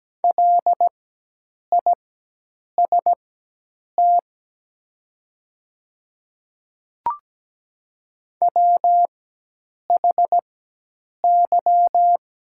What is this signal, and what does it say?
Morse code, heard as a keyed tone, about 17 wpm, spaced as for 8 wpm, 700 Hz. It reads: LIST WHY
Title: 4x Character Spacing - Top 500 Words - 17wpm